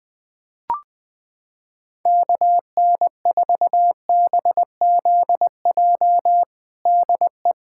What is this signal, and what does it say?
Morse code, heard as a keyed tone, about 20 wpm, 700 Hz. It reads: KN4BZJ DE
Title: USA Callsign DE Callsign - 6 Characters - 20wpm